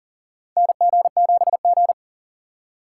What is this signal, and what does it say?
Morse code, heard as a keyed tone, about 40 wpm, 700 Hz. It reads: NG7G
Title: USA Call Signs - 4 Characters - 4x Character Spacing - 40wpm